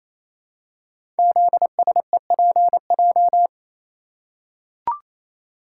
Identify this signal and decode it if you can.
Morse code, heard as a keyed tone, about 28 wpm, 700 Hz. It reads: ZSEPJ
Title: Code Groups - 5 Letters - 28wpm